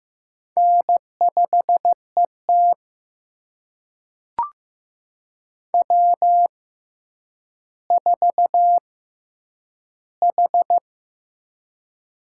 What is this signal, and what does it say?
Morse code, heard as a keyed tone, about 15 wpm, spaced as for 9 wpm, 700 Hz. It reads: N5ET W4H
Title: USA Call Signs - 4 Characters - 6x Character Spacing - 15wpm